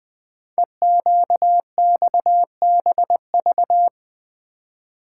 Morse code, heard as a keyed tone, about 20 wpm, 700 Hz. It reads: EQXBV
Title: Code Groups - 5 Letter:Numbers - 20wpm